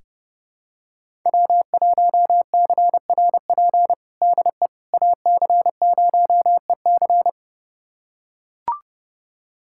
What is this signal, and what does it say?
Morse code, heard as a keyed tone, about 30 wpm, 700 Hz. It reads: W1CRP DE AC0EC
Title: USA Callsign DE Callsign - 5 Characters - 30wpm